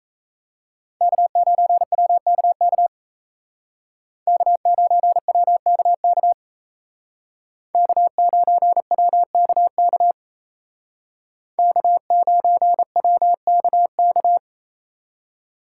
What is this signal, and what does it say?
Morse code, heard as a keyed tone, about 34 wpm, 700 Hz. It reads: K9WKK K9WKK K9WKK K9WKK
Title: Speed Racing - USA Call Signs 28wpm